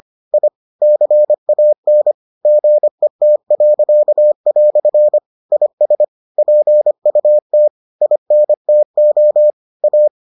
Morse code, heard as a keyed tone, about 25 wpm, 600 Hz. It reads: I CAN GET." IS PUT INTO A